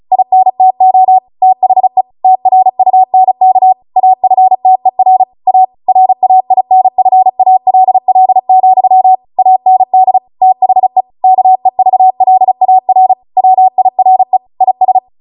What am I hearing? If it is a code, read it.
Morse code, heard as a keyed tone, about 36 wpm, 750 Hz. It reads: INTO THE TRUNK AFTER A RAINFALL, AND THE KEVLAR WIRE IS